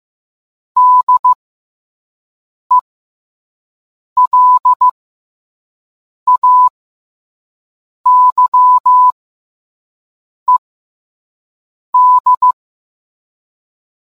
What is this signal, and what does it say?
Morse code, heard as a keyed tone, about 15 wpm, spaced as for 5 wpm, 1000 Hz. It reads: DELAYED